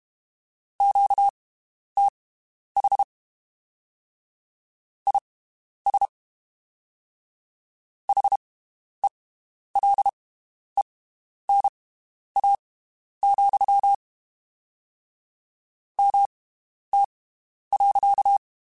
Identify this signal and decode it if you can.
Morse code, heard as a keyed tone, about 32 wpm, spaced as for 11 wpm, 800 Hz. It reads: QTH IS HELENA, MT.
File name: (spaced) / QSOs Spaced-mp3s / WD0BGP 1 @30 sp